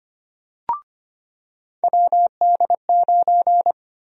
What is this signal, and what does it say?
Morse code, heard as a keyed tone, about 25 wpm, 700 Hz. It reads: WD9
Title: USA Call Sign Prefixes - 25wpm